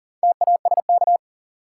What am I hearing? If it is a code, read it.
Morse code, heard as a keyed tone, about 40 wpm, 700 Hz. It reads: TASK
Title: Sentences from Top 1000 Words - Review Part 7 - New Word Drill - 40wpm